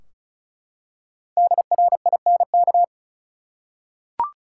Morse code, heard as a keyed tone, about 36 wpm, 700 Hz. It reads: DRINK
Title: Five Letter Words - 35wpm